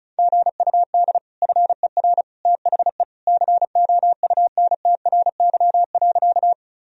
Morse code, heard as a keyed tone, about 36 wpm, 700 Hz. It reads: GUD FER THE COUNTRY.
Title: Sentences from Top 300 Words and Abbreviations 35wpm